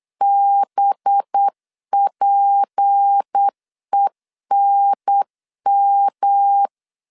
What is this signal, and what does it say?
Morse code, heard as a keyed tone, about 8 wpm, 800 Hz. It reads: BPENM
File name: 8wpm